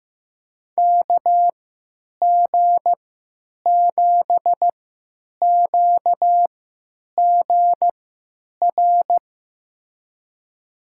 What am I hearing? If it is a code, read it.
Morse code, heard as a keyed tone, about 15 wpm, spaced as for 9 wpm, 700 Hz. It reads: KG7QGR
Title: USA Call Signs - 6 Characters - 3x Character Spacing - 15wpm